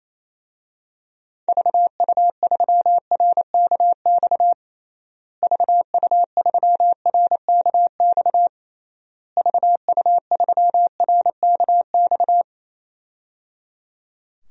Morse code, heard as a keyed tone, about 28 wpm, 700 Hz. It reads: VU3RKX VU3RKX VU3RKX
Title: DXCC-100 Call Signs - 6 Characters - Repeated 3x - 28wpm